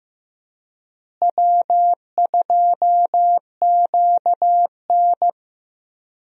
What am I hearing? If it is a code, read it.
Morse code, heard as a keyed tone, about 15 wpm, 700 Hz. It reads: W2QN
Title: USA Call Signs - 4 Characters - Repeated 3x - 15wpm